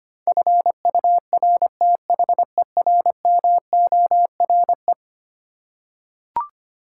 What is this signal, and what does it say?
Morse code, heard as a keyed tone, about 25 wpm, 700 Hz. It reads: FURTHERMORE
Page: morse-code-ninja-practice-25wpm